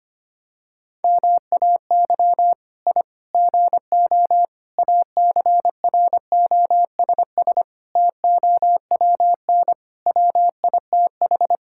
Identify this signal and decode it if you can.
Morse code, heard as a keyed tone, about 25 wpm, 700 Hz. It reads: MAY I GO ACROSS TOWN WITH
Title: Sentences from Top 1000 Words - Review Part 5 - 25wpm